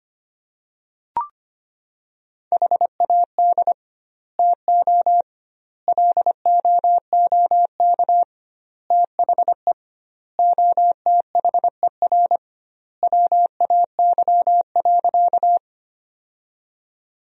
Morse code, heard as a keyed tone, about 25 wpm, 700 Hz. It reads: HAD TO LOOK THE OTHER WAY.
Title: 2x Word-Spacing - Sentences from Top 100 Words - 25wpm